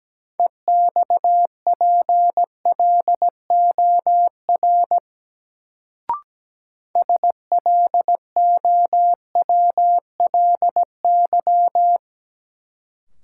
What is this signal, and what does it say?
Morse code, heard as a keyed tone, about 17 wpm, 700 Hz. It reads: EXPLOR SLOWLY
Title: Sentences from Top 1100 Words - New Word Drill - 17wpm